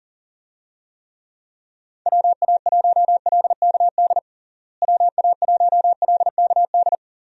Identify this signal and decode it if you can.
Morse code, heard as a keyed tone, about 40 wpm, 700 Hz. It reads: WA1LKD WA1LKD
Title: USA Call Signs - 6 Characters - Repeated 3x - 40wpm